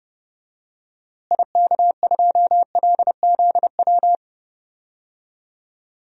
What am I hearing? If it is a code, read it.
Morse code, heard as a keyed tone, about 32 wpm, 700 Hz. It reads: IK2LZW
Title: DXCC-100 Call Signs - 6 Characters - Repeated 2x - 30wpm